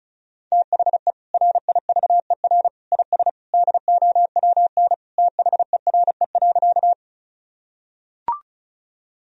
Morse code, heard as a keyed tone, about 34 wpm, 700 Hz. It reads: THE RIVER IS DOWN THERE.
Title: Sentences from Top 1000 Words - Review Part 1 - 35wpm